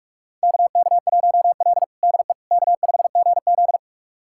Morse code, heard as a keyed tone, about 44 wpm, 690 Hz. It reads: KK1L DE K5GZ
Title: USA Callsign DE Callsign - 4 Characters - 45wpm